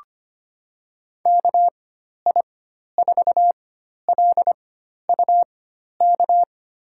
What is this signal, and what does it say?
Morse code, heard as a keyed tone, about 25 wpm, spaced as for 12 wpm, 700 Hz. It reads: KI4LUK
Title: USA Call Signs - 6 Characters - 4x Character Spacing - 25wpm